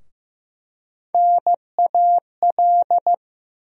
Morse code, heard as a keyed tone, about 15 wpm, 700 Hz. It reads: NAL
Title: USA Call Sign Postfixes - 3 Letters 15wpm